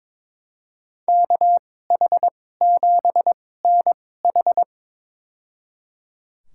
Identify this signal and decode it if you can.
Morse code, heard as a keyed tone, about 22 wpm, spaced as for 16 wpm, 700 Hz. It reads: KH7NH
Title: USA Call Signs - 5 Characters - 2x Character Spacing - 22wpm